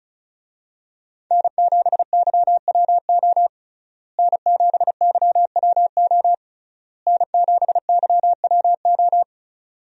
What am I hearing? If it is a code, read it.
Morse code, heard as a keyed tone, about 36 wpm, 700 Hz. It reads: N7YWO N7YWO N7YWO
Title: USA Call Signs - 5 Characters - Repeated 3x - 35wpm